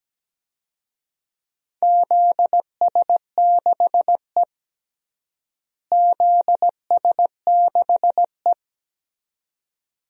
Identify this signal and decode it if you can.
Morse code, heard as a keyed tone, about 17 wpm, 700 Hz. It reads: ZS6E ZS6E
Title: DXCC-100 Call Signs - 4 Characters - Repeated 2x - 17wpm